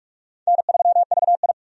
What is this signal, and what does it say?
Morse code, heard as a keyed tone, about 46 wpm, 700 Hz. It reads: N3VI
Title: USA Call Signs - 4 Characters - 45wpm